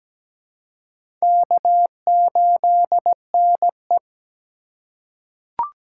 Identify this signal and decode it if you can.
Morse code, heard as a keyed tone, about 17 wpm, 700 Hz. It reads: K8NE
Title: USA Call Signs - 4 Characters - 4x Character Spacing - 17wpm